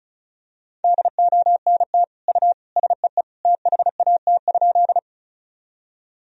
Morse code, heard as a keyed tone, about 34 wpm, 700 Hz. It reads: DONT U SEE THAT?
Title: Sentences from Top 300 Words and Abbreviations 35wpm